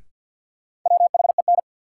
Morse code, heard as a keyed tone, about 50 wpm, 700 Hz. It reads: WHEN
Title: Sets of 2 Words - New Word Drill - 50wpm